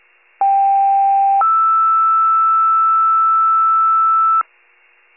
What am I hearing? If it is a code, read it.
Two-tone page, heard as a one-second tone, then a three-second tone, about 790 then 1320 Hz. It is heard from 0.4 to 4.4 s.